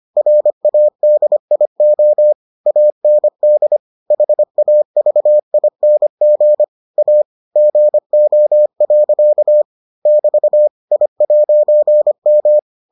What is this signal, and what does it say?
Morse code, heard as a keyed tone, about 25 wpm, 600 Hz. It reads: RADIO AND HAVING A GO. <BT> I'M